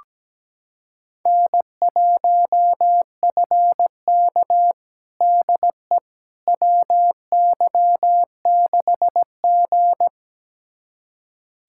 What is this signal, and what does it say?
Morse code, heard as a keyed tone, about 17 wpm, 700 Hz. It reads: N1FK DE WY6G
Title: USA Callsign DE Callsign - 4 Characters - 17wpm